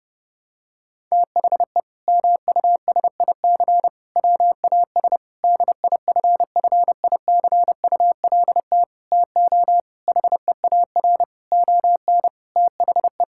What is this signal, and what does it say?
Morse code, heard as a keyed tone, about 30 wpm, 700 Hz. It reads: THE MUSIC WAS DIFFICULT TO HEAR ON THE
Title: Sentences from Top 1000 Words - Review Part 6 - 30wpm